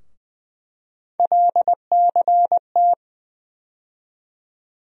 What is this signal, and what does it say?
Morse code, heard as a keyed tone, about 20 wpm, 700 Hz. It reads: LCT
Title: USA Call Sign Postfixes - 3 Letters 20wpm